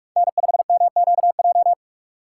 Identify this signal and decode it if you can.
Morse code, heard as a keyed tone, about 46 wpm, 690 Hz. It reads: N5MQJ
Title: USA Call Signs - 5 Characters - 6x Character Spacing - 45wpm